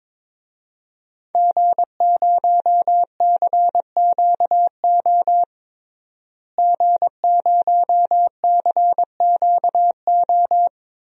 Morse code, heard as a keyed tone, about 21 wpm, 700 Hz. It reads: G0CQO G0CQO
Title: DXCC-100 Call Signs - 5 Characters - Repeated 2x - 22wpm